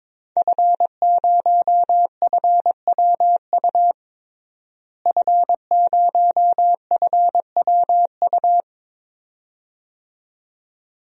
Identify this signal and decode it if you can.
Morse code, heard as a keyed tone, about 21 wpm, 700 Hz. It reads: F0FWU F0FWU
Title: DXCC-100 Call Signs - 5 Characters - Repeated 2x - 22wpm